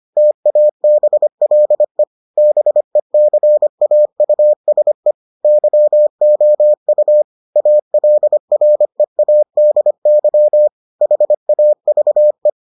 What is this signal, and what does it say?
Morse code, heard as a keyed tone, about 25 wpm, 600 Hz. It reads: TABLE BECAUSE YOU ALREADY HAVE